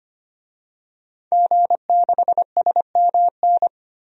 Morse code, heard as a keyed tone, about 25 wpm, 700 Hz. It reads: G6SMN
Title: DXCC-100 Call Signs - 5 Characters - 25wpm